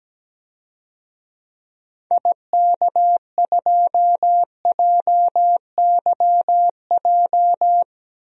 Morse code, heard as a keyed tone, about 17 wpm, 700 Hz. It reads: IK2JYJ
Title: DXCC-100 Call Signs - 6 Characters - 17wpm